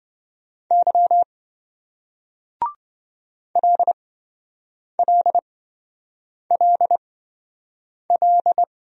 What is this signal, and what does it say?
Morse code, heard as a keyed tone, about 27 wpm, spaced as for 7 wpm, 700 Hz. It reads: Y LLLL